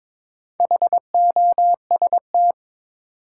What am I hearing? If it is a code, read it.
Morse code, heard as a keyed tone, about 21 wpm, 700 Hz. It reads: HOST
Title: Four Letter Words - 22wpm